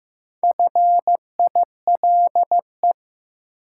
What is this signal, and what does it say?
Morse code, heard as a keyed tone, about 15 wpm, 700 Hz. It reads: FILE
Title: Sentences from Top 1000 Words - Review Part 4 - New Word Drill - 15wpm